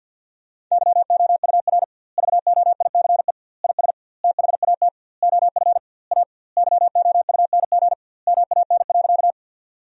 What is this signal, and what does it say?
Morse code, heard as a keyed tone, about 49 wpm, 700 Hz. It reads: YOUR VOICE IS THAT OF A YOUNG MAN.
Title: 2x Word-Spacing - Sentences from Top 500 Words - 50wpm